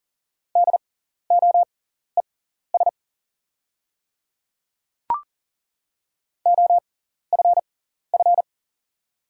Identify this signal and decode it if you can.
Morse code, heard as a keyed tone, about 40 wpm, spaced as for 14 wpm, 690 Hz. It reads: DOES OFF
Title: 6x Character Spacing - Top 200 Words - 40wpm